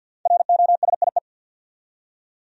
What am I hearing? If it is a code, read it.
Morse code, heard as a keyed tone, about 52 wpm, 700 Hz. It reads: ROSIE
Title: USA Common Names - Max 5 Letters - 50wpm